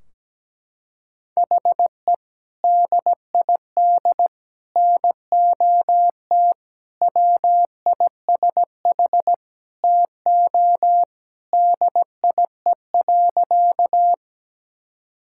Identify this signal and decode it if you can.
Morse code, heard as a keyed tone, about 17 wpm, 700 Hz. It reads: HE DID NOT WISH TO DIE.
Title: Sentences from Top 1600 Words 17wpm